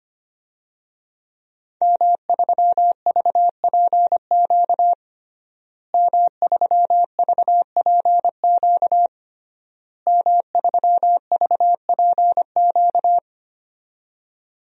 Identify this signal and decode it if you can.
Morse code, heard as a keyed tone, about 25 wpm, 700 Hz. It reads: M3VPQ M3VPQ M3VPQ
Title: DXCC-100 Call Signs - 5 Characters - Repeated 3x - 25wpm